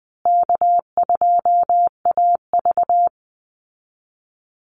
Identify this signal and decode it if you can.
Morse code, heard as a keyed tone, about 20 wpm, 700 Hz. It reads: K2AV
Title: Easy Call Signs Repeated 3x - 20wpm